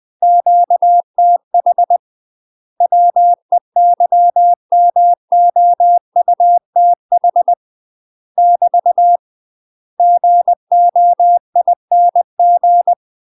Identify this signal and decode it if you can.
Morse code, heard as a keyed tone, about 20 wpm, 700 Hz. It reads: QTH WEYMOUTH <BT> GOING
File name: UK MASTER QSOs TRAINING/20WPM